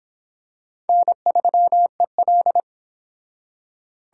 Morse code, heard as a keyed tone, about 25 wpm, 700 Hz. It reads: N3EL